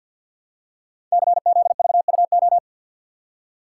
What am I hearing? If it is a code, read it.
Morse code, heard as a keyed tone, about 46 wpm, 700 Hz. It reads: CZVUO
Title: Code Groups - 5 Letter:Numbers - 50wpm